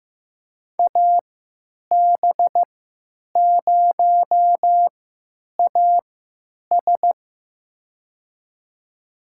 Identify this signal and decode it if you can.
Morse code, heard as a keyed tone, about 15 wpm, spaced as for 9 wpm, 700 Hz. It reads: AB0AS
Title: USA Call Signs - 3x Character Spacing - 15wpm